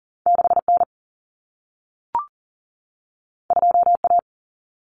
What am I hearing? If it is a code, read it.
Morse code, heard as a keyed tone, about 40 wpm, 700 Hz. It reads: 6N 2A
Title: Common Cut Numbers - 2-Digit 40wpm